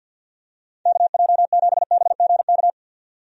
Morse code, heard as a keyed tone, about 51 wpm, 700 Hz. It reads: KJ7BGK
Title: USA Call Signs - 6 Characters - 50wpm